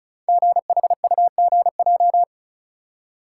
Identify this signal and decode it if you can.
Morse code, heard as a keyed tone, about 36 wpm, 690 Hz. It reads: GHUGJ
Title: Code Groups - 5 Letters - 35wpm